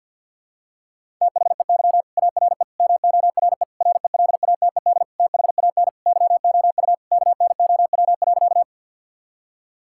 Morse code, heard as a keyed tone, about 52 wpm, 700 Hz. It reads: THEY ARE MORE RELATED THAN YOU KNOW.
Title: Sentences from Top 1000 Words - Review Part 5 - 50wpm